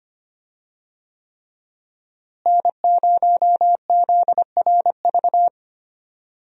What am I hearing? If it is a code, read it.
Morse code, heard as a keyed tone, about 25 wpm, 700 Hz. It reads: N0ZRV
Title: USA Call Signs - 5 Characters - Repeated 3x - 25wpm